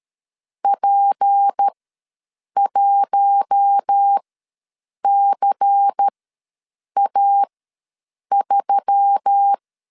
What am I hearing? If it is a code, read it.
Morse code, heard as a keyed tone, about 13 wpm, spaced as for 7 wpm, 800 Hz. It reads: P1CA3
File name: LEZIONI / 7 GF9Z / 7 G ws 12wpm 3140